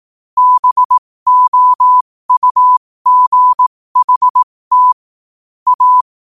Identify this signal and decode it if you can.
Morse code, heard as a keyed tone, about 18 wpm, spaced as for 15 wpm, 1000 Hz. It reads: BOUGHT A
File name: qotdfiles / 15wpm